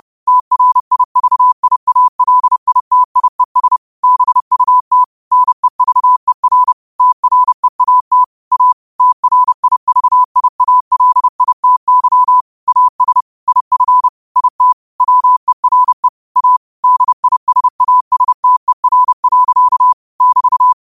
Morse code, heard as a keyed tone, about 30 wpm, 1000 Hz. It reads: TRIVIALITIES BUT NEVER TREAT A TRIVIALITY AS IF IT WERE A DISASTER. <BT>
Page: New quotes every day in morse code at 30 Words per minute.